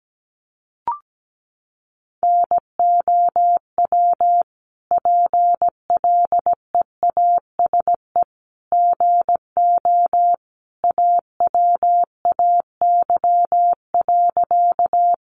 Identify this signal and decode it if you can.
Morse code, heard as a keyed tone, about 17 wpm, 700 Hz. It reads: NOW PLEASE GO AWAY.
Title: Sentences from Top 1500 Words 17wpm